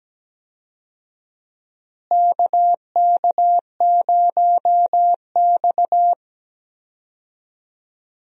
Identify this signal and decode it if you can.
Morse code, heard as a keyed tone, about 17 wpm, 700 Hz. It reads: KK0X
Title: USA Call Signs - 4 Characters - Repeated 3x - 17wpm